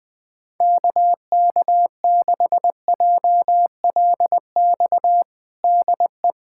Morse code, heard as a keyed tone, about 20 wpm, 700 Hz. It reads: KK6JLX DE
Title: USA Callsign DE Callsign 20wpm